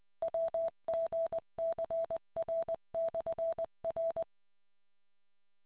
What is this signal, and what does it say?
Morse code, heard as a keyed tone, about 20 wpm, 660 Hz. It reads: WGCR/R